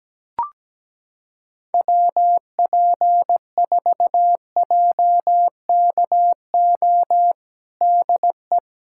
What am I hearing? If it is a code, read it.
Morse code, heard as a keyed tone, about 17 wpm, 700 Hz. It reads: WP4JKO DE
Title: USA Callsign DE Callsign 17wpm